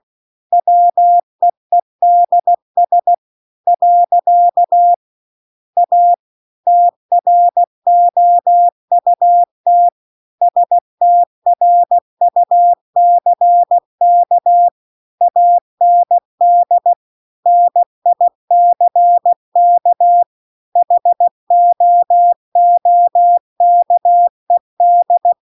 Morse code, heard as a keyed tone, about 16 wpm, 700 Hz. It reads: WEEDS . A TROUT STRUCK AND NICK HOOKED
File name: CW-The-Big-Twoheaded-River-2-16WPM